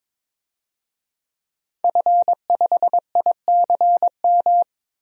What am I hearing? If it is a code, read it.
Morse code, heard as a keyed tone, about 21 wpm, 700 Hz. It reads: F5ICM
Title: DXCC-100 Call Signs - 5 Characters - 22wpm